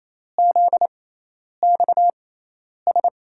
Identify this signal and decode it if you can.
Morse code, heard as a keyed tone, about 28 wpm, spaced as for 10 wpm, 700 Hz. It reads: ZXS
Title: USA Call Signs - 6x Character Spacing - 28wpm